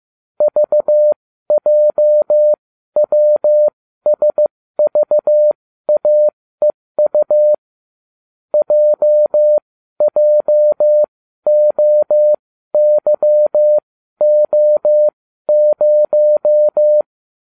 Morse code, heard as a keyed tone, about 15 wpm, spaced as for 12 wpm, 600 Hz. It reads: VJWSVAEU JJOYO0